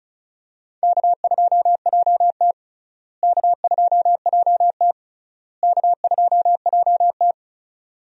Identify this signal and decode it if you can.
Morse code, heard as a keyed tone, about 36 wpm, 700 Hz. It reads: K2JT K2JT K2JT
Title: USA Call Signs - 4 Characters - Repeated 3x - 35wpm